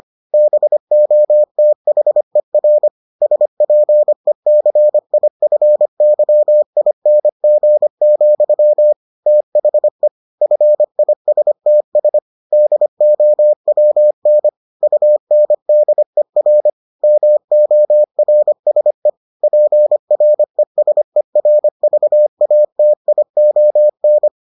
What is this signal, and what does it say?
Morse code, heard as a keyed tone, about 25 wpm, 600 Hz. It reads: BOTHER SPECIFYING, THE FISTS DOWN UNDER MORSE PRESERVATION